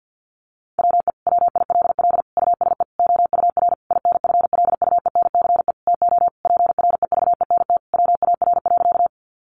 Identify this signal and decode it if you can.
Morse code, heard as a keyed tone, about 48 wpm, 700 Hz. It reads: WE WILL USE OUR INFLUENCE TO PREVENT WAR.
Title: Sentences from Top 1100 Words 50wpm